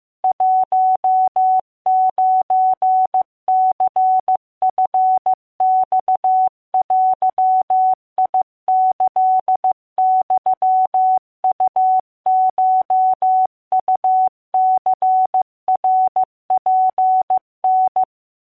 Morse code (Japanese Wabun code), heard as a keyed tone, about 15 wpm, 750 Hz. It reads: 19ニチマデキユウコウニナツタ